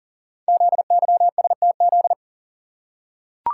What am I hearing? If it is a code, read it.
Morse code, heard as a keyed tone, about 38 wpm, 700 Hz. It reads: ZYSTZ
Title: Code Groups - 5 Letters - 40wpm